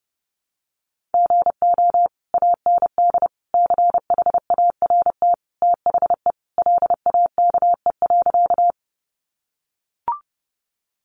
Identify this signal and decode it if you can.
Morse code, heard as a keyed tone, about 30 wpm, 700 Hz. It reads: GO AND CHART THE LAKE.
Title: Sentences from Top 1300 Words 30wpm